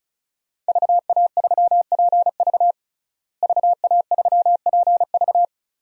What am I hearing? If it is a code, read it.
Morse code, heard as a keyed tone, about 34 wpm, 700 Hz. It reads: VA3PV VA3PV
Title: DXCC-100 Call Signs - 5 Characters - Repeated 3x - 35wpm